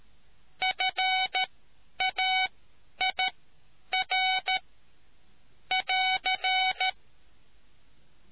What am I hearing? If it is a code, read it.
Morse code, heard as a keyed tone, about 13 wpm, spaced as for 9 wpm, 750 Hz. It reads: FAIR <AR>